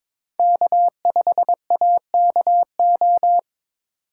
Morse code, heard as a keyed tone, about 22 wpm, 700 Hz. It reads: K5AKO
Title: USA Call Signs - 3x Character Spacing - 22wpm